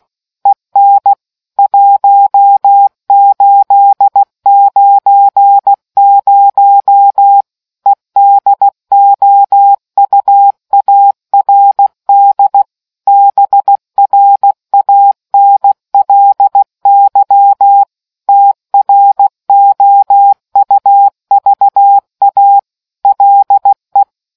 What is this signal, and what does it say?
Morse code, heard as a keyed tone, about 16 wpm, 800 Hz. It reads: EN 1890 EDOUARD BRANLY TROUVA LE